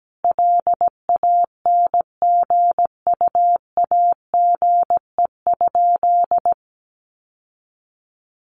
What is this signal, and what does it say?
Morse code, heard as a keyed tone, about 17 wpm, 700 Hz. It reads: LANGUAGE?
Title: Sentences from Top 900 Words 17wpm